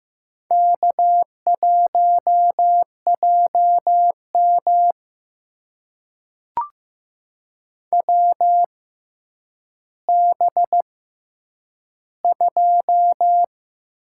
Morse code, heard as a keyed tone, about 16 wpm, spaced as for 9 wpm, 700 Hz. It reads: K1JM WB2
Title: USA Call Signs - 6x Character Spacing - 15wpm